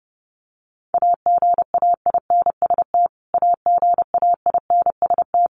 Morse code, heard as a keyed tone, about 30 wpm, 700 Hz. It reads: AGAINST AGAINST
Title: Top 500 Words Repeated 30wpm